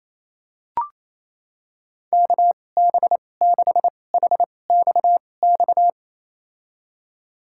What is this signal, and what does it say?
Morse code, heard as a keyed tone, about 28 wpm, spaced as for 20 wpm, 700 Hz. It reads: KB6HXX
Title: USA Call Signs - 6 Characters - 2x Character Spacing - 28wpm